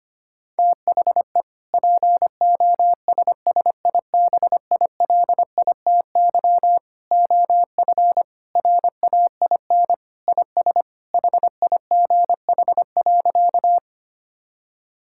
Morse code, heard as a keyed tone, about 25 wpm, 700 Hz. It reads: THE POSSIBILITY OF RAIN IS HIGH.
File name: practice / 25wpm / Sentences with 3-Word Phrases - 25wpm